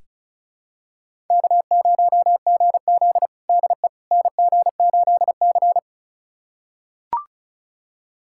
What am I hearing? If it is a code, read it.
Morse code, heard as a keyed tone, about 36 wpm, 700 Hz. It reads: K0GZ DE NG8C
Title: USA Callsign DE Callsign - 4 Characters - 35wpm